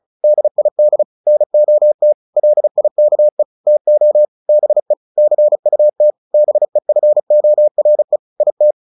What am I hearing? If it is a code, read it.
Morse code, heard as a keyed tone, about 34 wpm, 600 Hz. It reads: DID NOT LIKE TO BE CUT BEFORE IT
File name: the old man and the sea / the old man and the sea 1 35wpm